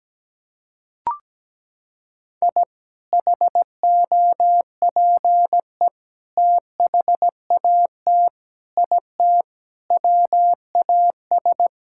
Morse code, heard as a keyed tone, about 17 wpm, 700 Hz. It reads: I HOPE THAT IT WAS
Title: Sentences from Top 1000 Words - Review Part 5 - 17wpm